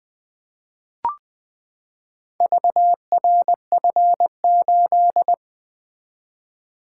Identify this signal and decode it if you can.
Morse code, heard as a keyed tone, about 20 wpm, 700 Hz. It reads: VRF8
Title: Four Letter-Number Combinations - 20wpm